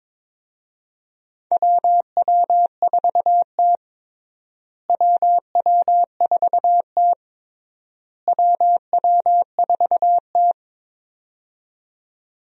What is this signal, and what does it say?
Morse code, heard as a keyed tone, about 22 wpm, 700 Hz. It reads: WW4T WW4T WW4T
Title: USA Call Signs - 4 Characters - Repeated 3x - 22wpm